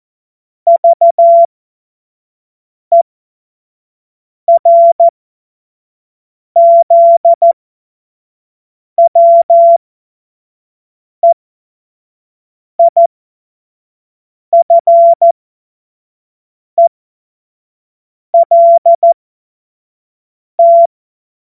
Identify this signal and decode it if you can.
Morse code, heard as a keyed tone, about 14 wpm, spaced as for 5 wpm, 680 Hz. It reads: VERZWEIFELT
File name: Morsekurs / Witze / Witz 23. August 2022 25 F70